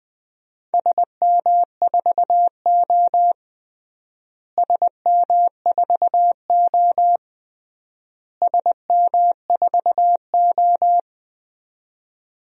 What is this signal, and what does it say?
Morse code, heard as a keyed tone, about 20 wpm, 700 Hz. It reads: SM4O SM4O SM4O
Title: DXCC-100 Call Signs - 4 Characters - Repeated 3x - 20wpm